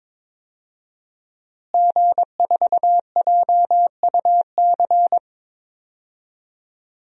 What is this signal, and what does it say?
Morse code, heard as a keyed tone, about 22 wpm, 700 Hz. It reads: G4JUC
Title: DXCC-100 Call Signs - 5 Characters - Repeated 3x - 22wpm